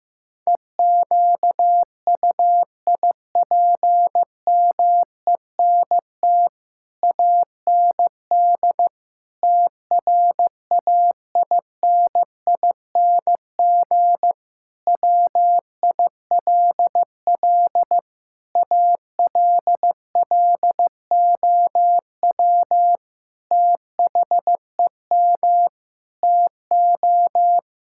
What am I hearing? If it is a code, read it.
Morse code, heard as a keyed tone, about 14 wpm, 700 Hz. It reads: EQUIPMENT AND TRAINING WILL ALLOW THEM TO
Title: Sentences from Top 1000 Words - Review Part 5 - 15wpm